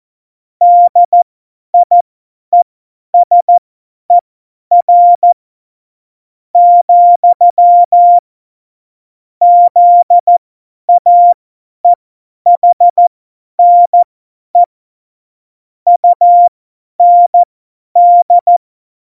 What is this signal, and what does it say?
Morse code, heard as a keyed tone, about 14 wpm, spaced as for 9 wpm, 700 Hz. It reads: DIESER, ZAEHNE UND
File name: Morsekurs / Witze / Witz 02. Januar 2018, 50 F70